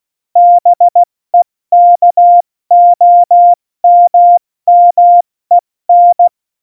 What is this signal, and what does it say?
Morse code, heard as a keyed tone, about 16 wpm, 700 Hz. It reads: BEKOMMEN
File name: Morsekurs / Witze / Witz 02. Januar 2018, 70 F80